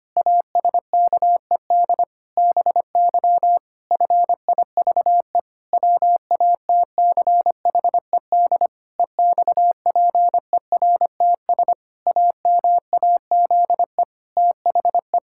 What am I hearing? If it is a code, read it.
Morse code, heard as a keyed tone, about 25 wpm, 700 Hz. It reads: ASKED BY FIVE WATCHED EXPERTS AMAZE THE
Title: Pangram Challenge v2 - 25wpm